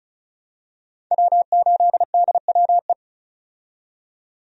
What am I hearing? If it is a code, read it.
Morse code, heard as a keyed tone, about 34 wpm, 700 Hz. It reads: W8DWE
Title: USA Call Signs - 5 Characters - 35wpm